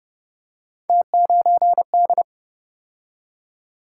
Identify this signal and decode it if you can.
Morse code, heard as a keyed tone, about 35 wpm, 700 Hz. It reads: T9D